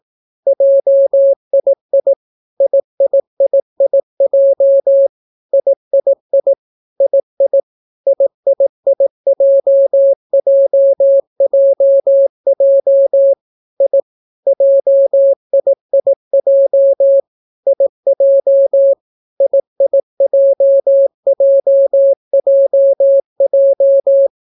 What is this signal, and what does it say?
Morse code, heard as a keyed tone, about 18 wpm, 550 Hz. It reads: JII IIIIJ III II IIIJJJJ I JIIJ IJ IIJJJJ